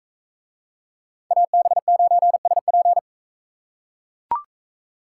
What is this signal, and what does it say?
Morse code, heard as a keyed tone, about 41 wpm, 700 Hz. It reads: AB9SP